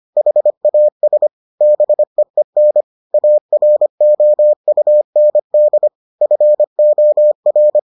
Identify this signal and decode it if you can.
Morse code, heard as a keyed tone, about 25 wpm, 600 Hz. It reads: HAS BEEN AROUND FOR